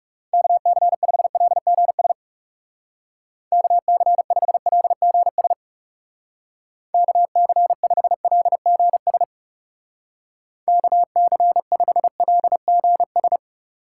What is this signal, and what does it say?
Morse code, heard as a keyed tone, about 36 wpm, 700 Hz. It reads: KC5LGS KC5LGS KC5LGS KC5LGS
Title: Speed Racing - USA 2x3 Call Signs 30wpm